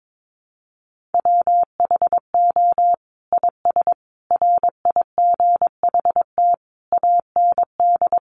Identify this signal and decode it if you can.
Morse code, heard as a keyed tone, about 22 wpm, 700 Hz. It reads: WHO IS RIGHT AND
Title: Sentences from Top 900 Words 22wpm